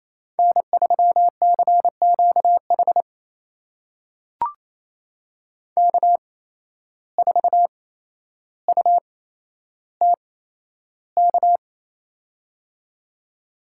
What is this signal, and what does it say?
Morse code, heard as a keyed tone, about 28 wpm, spaced as for 12 wpm, 700 Hz. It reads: N3CQH K4UTK
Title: USA Call Signs - 5 Characters - 8x Character Spacing - 28wpm